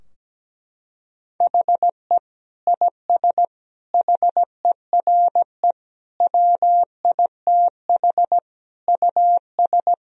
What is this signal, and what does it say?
Morse code, heard as a keyed tone, about 17 wpm, 700 Hz. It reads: HE IS HERE WITH US
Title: Sentences from Top 1000 Words - Review Part 3 - 17wpm